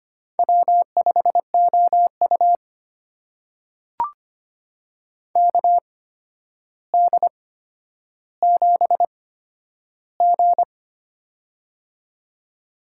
Morse code, heard as a keyed tone, about 25 wpm, spaced as for 11 wpm, 700 Hz. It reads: W5OU KD7G
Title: USA Call Signs - 4 Characters - 8x Character Spacing - 25wpm